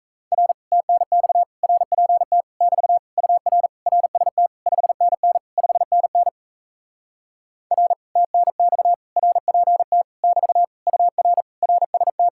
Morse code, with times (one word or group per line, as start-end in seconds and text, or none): 0.3-0.5 R
0.7-1.4 TNX
1.6-2.4 RPT
2.6-3.0 <BT>
3.2-3.7 UR
3.9-4.5 RST
4.7-5.4 5NN
5.6-6.3 5NN
7.7-7.9 R
8.2-8.9 TNX
9.2-10.0 RPT
10.2-10.6 <BT>
10.9-11.4 UR
11.6-12.3 RST